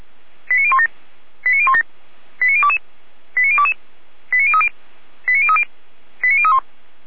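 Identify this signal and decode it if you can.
Five-tone selective call (ZVEI), heard as quick runs of five tones, tones of about 80 ms. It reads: 89017 89017 89022 89022 89030 89030 89031
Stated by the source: ZVEI five-tone call 89012 received on SDR.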